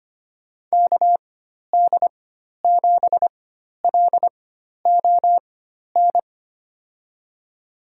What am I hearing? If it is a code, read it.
Morse code, heard as a keyed tone, about 25 wpm, spaced as for 12 wpm, 700 Hz. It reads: KD7LON
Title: USA Call Signs - 6 Characters - 4x Character Spacing - 25wpm